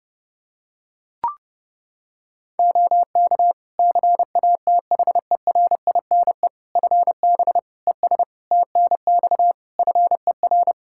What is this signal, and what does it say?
Morse code, heard as a keyed tone, about 32 wpm, 700 Hz. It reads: OK CATHERINE FB ES TNX FER
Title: Practice QSO - Exchange 2 - Info - 30wpm